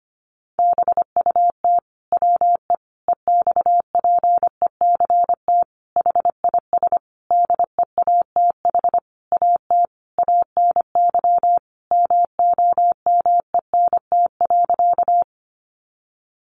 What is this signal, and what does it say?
Morse code, heard as a keyed tone, about 25 wpm, 700 Hz. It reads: BUT WE EXPECT HIS DEATH AT ANY MOMENT.
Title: Sentences from Top 1100 Words 25wpm